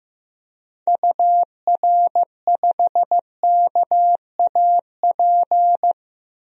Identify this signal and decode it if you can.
Morse code, heard as a keyed tone, about 15 wpm, 700 Hz. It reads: UR5KAP
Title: DXCC-100 Call Signs - 6 Characters - Repeated 3x - 15wpm